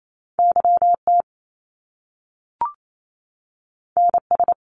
Morse code, heard as a keyed tone, about 28 wpm, 700 Hz. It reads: YT NS
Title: Canadian Provinces and Territories 28wpm